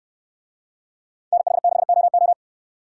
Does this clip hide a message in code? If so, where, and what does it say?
Yes: Morse code from 1.3 to 2.3 s, heard as a keyed tone, about 69 wpm, 700 Hz. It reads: NH6OO